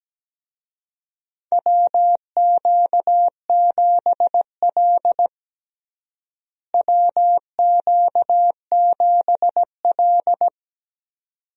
Morse code, heard as a keyed tone, about 17 wpm, 700 Hz. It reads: WQ7L WQ7L
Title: USA Call Signs - 4 Characters - Repeated 2x - 17wpm